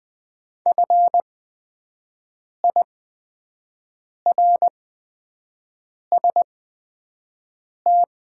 Morse code, heard as a keyed tone, about 20 wpm, spaced as for 5 wpm, 700 Hz. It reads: FIRST
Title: 8x Character Spacing - Top 100 Words - 20wpm